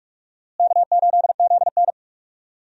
Morse code, heard as a keyed tone, about 46 wpm, 700 Hz. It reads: K8ZN
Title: USA Call Signs - 4 Characters - Repeated 3x - 45wpm